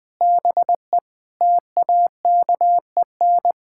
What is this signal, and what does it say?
Morse code, heard as a keyed tone, about 21 wpm, 700 Hz. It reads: BE TAKEN
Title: Sentences with 7-Word Phrases - Spoken Beginning - 20wpm